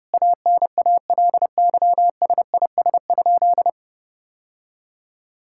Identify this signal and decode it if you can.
Morse code, heard as a keyed tone, about 32 wpm, 700 Hz. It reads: ANALYSIS?
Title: Sentences from Top 1000 Words - Review Part 5 - 30wpm